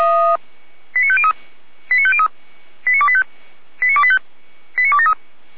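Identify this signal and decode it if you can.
Five-tone selective call (ZVEI), heard as quick runs of five tones, tones of about 70 ms. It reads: 89503 89503 89285 89285 89273
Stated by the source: ZVEI five-tone call 89108 received by SDR, from audio.f85360.feuerwehr-orlamuende.org fire brigade (Feuerwehr Orlamünde) archive.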